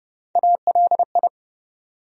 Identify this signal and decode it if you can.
Morse code, heard as a keyed tone, about 30 wpm, 700 Hz. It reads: ALI
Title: USA Call Sign Postfixes - 3 Letters 30wpm